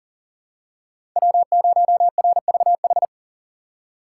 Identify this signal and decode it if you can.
Morse code, heard as a keyed tone, about 40 wpm, 700 Hz. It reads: W0RVH
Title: USA Call Signs - 5 Characters - 6x Character Spacing - 40wpm